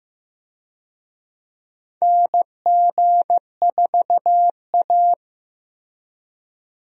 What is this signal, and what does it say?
Morse code, heard as a keyed tone, about 15 wpm, 700 Hz. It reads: NG4A
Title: USA Call Signs - 4 Characters - Repeated 3x - 15wpm